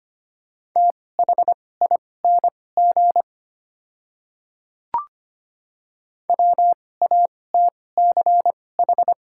Morse code, heard as a keyed tone, about 25 wpm, spaced as for 18 wpm, 700 Hz. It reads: THING WATCH
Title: 2x Character Spacing - Top 300 Words - 25wpm